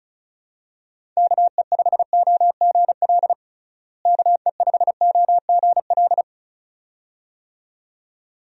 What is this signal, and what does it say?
Morse code, heard as a keyed tone, about 36 wpm, 700 Hz. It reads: KE5OGL KE5OGL
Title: USA Call Signs - 6 Characters - Repeated 2x - 35wpm